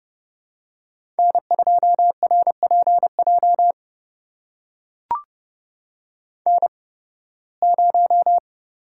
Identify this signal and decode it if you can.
Morse code, heard as a keyed tone, about 30 wpm, spaced as for 20 wpm, 700 Hz. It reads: N2RPJ N0
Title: USA Call Signs - 5 Characters - 8x Character Spacing - 30wpm